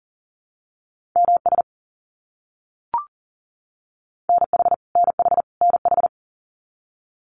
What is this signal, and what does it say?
Morse code, heard as a keyed tone, about 40 wpm, 700 Hz. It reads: MS NH NH NH
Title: US States Repeated 3x 40wpm